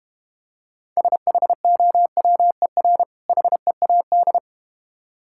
Morse code, heard as a keyed tone, about 31 wpm, 700 Hz. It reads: SHOWER HEAD
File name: Easy Words - Speed-Racing with Hints - 25wpm